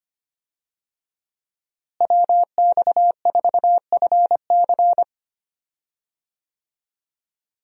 Morse code, heard as a keyed tone, about 25 wpm, 700 Hz. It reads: WX4FC
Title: USA Call Signs - 5 Characters - Repeated 3x - 25wpm